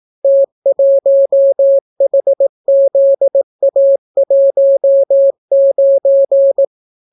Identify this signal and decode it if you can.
Morse code, heard as a keyed tone, about 18 wpm, 550 Hz. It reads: T1HZA19